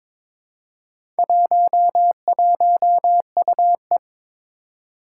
Morse code, heard as a keyed tone, about 22 wpm, 700 Hz. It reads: 11UE
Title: Four Challenging Letter-Number Combinations 22wpm